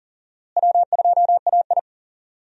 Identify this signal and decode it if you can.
Morse code, heard as a keyed tone, about 40 wpm, 700 Hz. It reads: W2AI
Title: USA Call Signs - 4 Characters - Repeated 3x - 40wpm